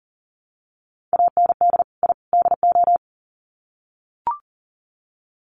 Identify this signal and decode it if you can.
Morse code, heard as a keyed tone, about 40 wpm, 700 Hz. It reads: AND I DO
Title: Sets of 3 Words 40wpm